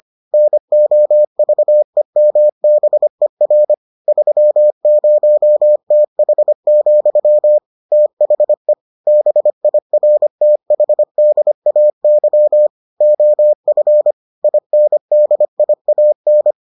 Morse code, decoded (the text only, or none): NOVEMBER 30TH, THE BIRTHDAY OF INDIAN